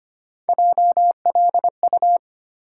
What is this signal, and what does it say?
Morse code, heard as a keyed tone, about 25 wpm, 700 Hz. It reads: JLU
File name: Three Letter-Number Combinations - 25wpm